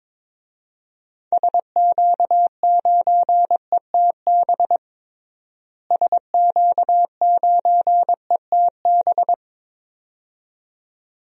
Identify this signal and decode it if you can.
Morse code, heard as a keyed tone, about 22 wpm, 700 Hz. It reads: SQ9ETB SQ9ETB
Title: DXCC-100 Call Signs - 6 Characters - Repeated 2x - 22wpm